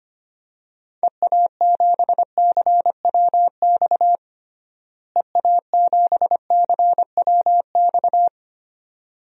Morse code, heard as a keyed tone, about 25 wpm, 700 Hz. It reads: EA7CWX EA7CWX
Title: DXCC-100 Call Signs - 6 Characters - Repeated 2x - 25wpm